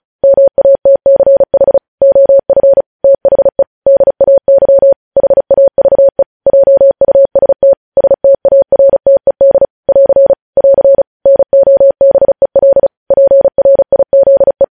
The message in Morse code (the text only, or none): MATCH OF THE DAY HAVE JUST STARTED <AR> <AR> NOBEL PRIZE